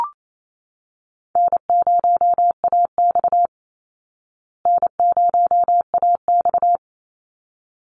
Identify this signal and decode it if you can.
Morse code, heard as a keyed tone, about 28 wpm, 700 Hz. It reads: N0AX N0AX
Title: Easy Call Signs Repeated 2x - 28wpm